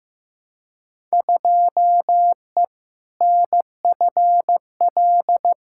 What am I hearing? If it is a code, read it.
Morse code, heard as a keyed tone, about 15 wpm, 700 Hz. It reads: 2E NFL
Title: Field Day Exchanges Encore - 15wpm